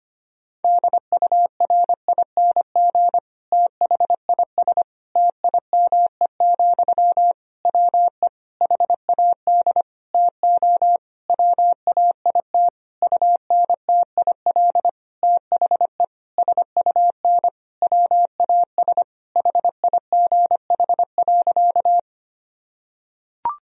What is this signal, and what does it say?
Morse code, heard as a keyed tone, about 25 wpm, 700 Hz. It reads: DURING THIS TIME, WE HAD TO WAIT UNTIL THE SUN WAS HIGH.
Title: Sentences from Top 1000 Words - Review Part 2 - 25wpm